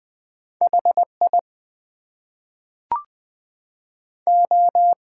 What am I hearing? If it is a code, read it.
Morse code, heard as a keyed tone, about 20 wpm, 700 Hz. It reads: HI O
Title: Two Letter Words - 20wpm